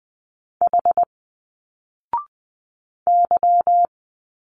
Morse code, heard as a keyed tone, about 20 wpm, 700 Hz. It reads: H Y